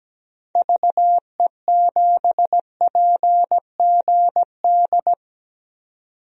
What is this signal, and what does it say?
Morse code, heard as a keyed tone, about 17 wpm, 700 Hz. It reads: VE7PGD
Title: DXCC-100 Call Signs - 6 Characters - 17wpm